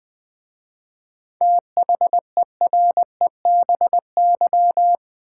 Morse code, heard as a keyed tone, about 20 wpm, 700 Hz. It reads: THEREBY
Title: 7 Letter Words - First 3 Letters Given - 20wpm